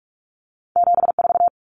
Morse code, heard as a keyed tone, about 46 wpm, 700 Hz. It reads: 74
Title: Common Cut Numbers - 2-Digit 45wpm